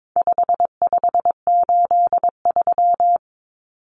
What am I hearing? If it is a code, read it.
Morse code, heard as a keyed tone, about 22 wpm, 700 Hz. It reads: 5583
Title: Common Cut Numbers - 4-Digit 22wpm